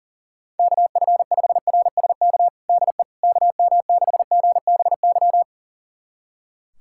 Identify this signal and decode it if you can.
Morse code, heard as a keyed tone, about 40 wpm, 700 Hz. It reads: KF5RSK DE KM6GBY
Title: USA Callsign DE Callsign - 6 Characters - 40wpm